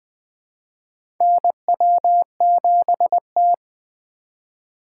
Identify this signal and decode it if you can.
Morse code, heard as a keyed tone, about 20 wpm, 700 Hz. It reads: NW7T
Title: USA Call Signs - 4 Characters - 4x Character Spacing - 20wpm